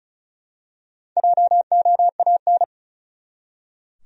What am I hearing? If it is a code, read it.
Morse code, heard as a keyed tone, about 35 wpm, 700 Hz. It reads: JOAN
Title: USA Common Names - Max 5 Letters - 35wpm